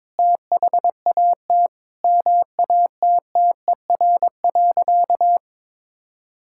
Morse code, heard as a keyed tone, about 22 wpm, 700 Hz. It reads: THAT MATTER.
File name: Sentences with 4-Word Phrases - 22wpm